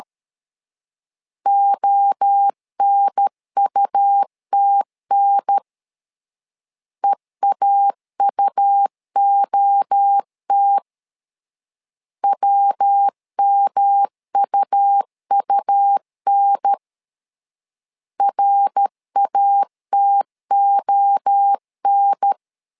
Morse code, heard as a keyed tone, about 13 wpm, 800 Hz. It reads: ONUTN EAUOT WMUUN RATON
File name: LEZIONI / 3 MWRO / 3 MWRO 12wpm 1730